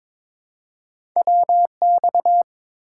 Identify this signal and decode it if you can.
Morse code, heard as a keyed tone, about 22 wpm, 700 Hz. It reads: WX
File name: Top 90 QSO Elements - 4x Character Spacing - 22wpm